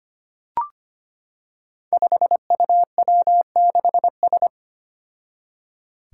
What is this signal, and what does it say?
Morse code, heard as a keyed tone, about 25 wpm, 700 Hz. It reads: 5UW6S
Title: Five Challenging Letter-Number Combinations 25wpm